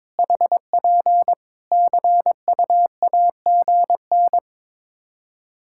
Morse code, heard as a keyed tone, about 22 wpm, 700 Hz. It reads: HP CUAGN
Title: Practice QSO - Exchange 4 - Ending - 22wpm